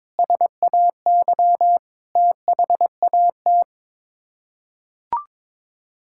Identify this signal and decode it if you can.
Morse code, heard as a keyed tone, about 22 wpm, 700 Hz. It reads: SAY THAT
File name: practice / 22wpm / Sets of 8 Words - 22wpm